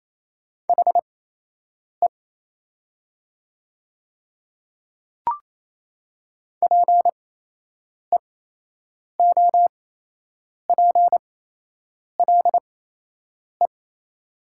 Morse code, heard as a keyed tone, about 28 wpm, spaced as for 8 wpm, 700 Hz. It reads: HE PEOPLE
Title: 8x Character Spacing - Top 100 Words - 28wpm